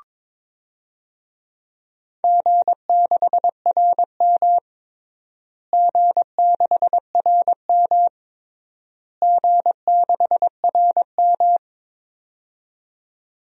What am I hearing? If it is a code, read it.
Morse code, heard as a keyed tone, about 21 wpm, 700 Hz. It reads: G6RM G6RM G6RM
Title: DXCC-100 Call Signs - 4 Characters - Repeated 3x - 22wpm